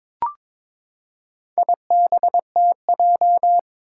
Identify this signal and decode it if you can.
Morse code, heard as a keyed tone, about 22 wpm, 700 Hz. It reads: IBTJ
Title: Four Challenging Letter-Number Combinations 22wpm